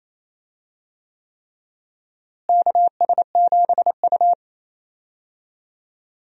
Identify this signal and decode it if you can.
Morse code, heard as a keyed tone, about 28 wpm, 700 Hz. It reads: KS7U
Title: USA Call Signs - 4 Characters - Repeated 3x - 28wpm